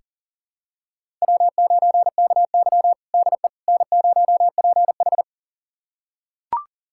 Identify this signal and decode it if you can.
Morse code, heard as a keyed tone, about 40 wpm, 700 Hz. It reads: W9KY DE N0PH
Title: USA Callsign DE Callsign - 4 Characters - 40wpm